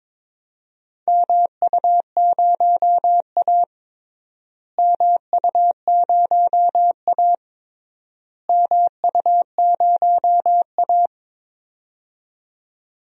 Morse code, heard as a keyed tone, about 21 wpm, 700 Hz. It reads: MU0A MU0A MU0A
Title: DXCC-100 Call Signs - 4 Characters - Repeated 3x - 22wpm